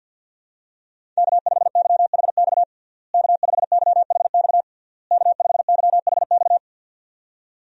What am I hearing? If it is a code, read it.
Morse code, heard as a keyed tone, about 52 wpm, 700 Hz. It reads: K5YHX K5YHX K5YHX
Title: USA Call Signs - 5 Characters - Repeated 3x - 50wpm